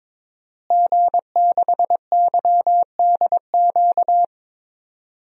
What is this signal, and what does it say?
Morse code, heard as a keyed tone, about 22 wpm, 700 Hz. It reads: G6YDQ
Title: DXCC-100 Call Signs - 5 Characters - Repeated 2x - 22wpm